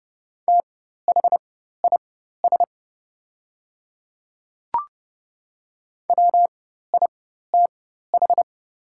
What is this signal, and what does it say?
Morse code, heard as a keyed tone, about 30 wpm, spaced as for 14 wpm, 700 Hz. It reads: THIS WITH
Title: 4x Character Spacing - Top 100 Words - 30wpm